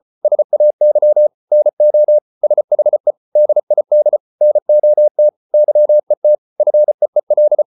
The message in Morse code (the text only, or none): SAY NO SHE DID NOT YET FEEL